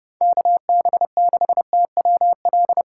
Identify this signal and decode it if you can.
Morse code, heard as a keyed tone, about 30 wpm, 700 Hz. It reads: KB6TWL
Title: USA Call Signs - 3x Character Spacing - 30wpm